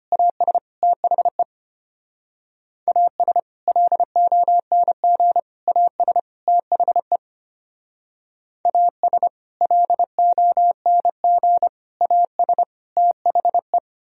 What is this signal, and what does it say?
Morse code, heard as a keyed tone, about 32 wpm, 700 Hz. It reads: AS THE AS LONG AS THE AS LONG AS THE